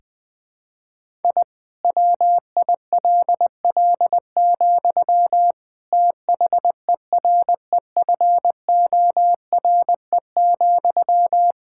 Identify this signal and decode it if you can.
Morse code, heard as a keyed tone, about 20 wpm, 700 Hz. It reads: I WILL, THEREFORE,
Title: Sentences from Top 1000 Words - Review Part 5 - 20wpm